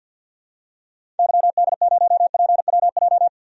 Morse code, heard as a keyed tone, about 48 wpm, 700 Hz. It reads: YD0PWJ
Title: DXCC-100 Call Signs - 6 Characters - 50wpm